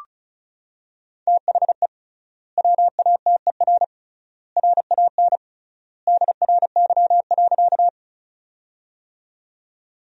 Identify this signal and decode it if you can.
Morse code, heard as a keyed tone, about 34 wpm, 700 Hz. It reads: THE WATER RAN DRY.
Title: 3x Word-Spacing - Sentences from Top 500 Words - 35wpm